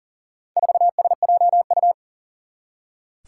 Morse code, heard as a keyed tone, about 40 wpm, 700 Hz. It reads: 4SJU
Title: Four Challenging Letter-Number Combinations 40wpm